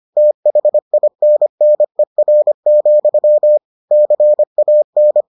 Morse code, heard as a keyed tone, about 25 wpm, 600 Hz. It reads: THINNER, CAN